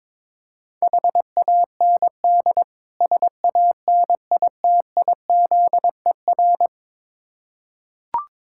Morse code, heard as a keyed tone, about 23 wpm, 700 Hz. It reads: HAND SANITIZER
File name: practice / 22wpm / Difficult Words with Hints - 22wpm